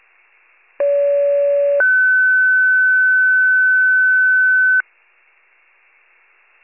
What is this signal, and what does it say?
Two-tone page, heard as a one-second tone, then a three-second tone, about 570 then 1500 Hz. It is heard from 0.8 to 4.8 s.